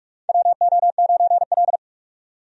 Morse code, heard as a keyed tone, about 41 wpm, 700 Hz. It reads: WO9L
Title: USA Call Signs - 4 Characters - 45wpm